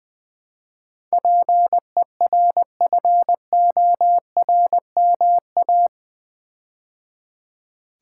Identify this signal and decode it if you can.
Morse code, heard as a keyed tone, about 20 wpm, 700 Hz. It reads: PERFORMA
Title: Build A Word - 4 Letters At A Time - 11-Letter Words - 20wpm